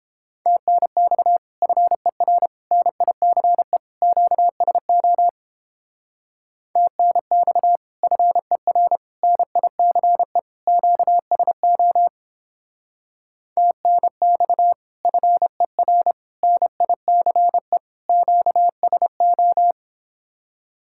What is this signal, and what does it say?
Morse code, heard as a keyed tone, about 31 wpm, 700 Hz. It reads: TNX FER NICE QSO TNX FER NICE QSO TNX FER NICE QSO